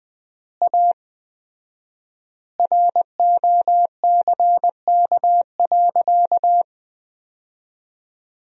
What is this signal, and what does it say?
Morse code, heard as a keyed tone, about 21 wpm, 700 Hz. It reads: A ROCK.
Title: 4x Word-Spacing - Sentences from Top 400 Words - 20wpm